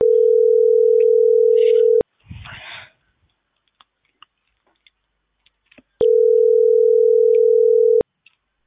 Ringback tone (North America), 440 + 480 Hz.